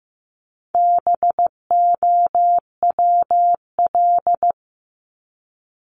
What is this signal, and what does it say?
Morse code, heard as a keyed tone, about 15 wpm, 700 Hz. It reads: BOWL
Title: Top 1001-1250 Words 15wpm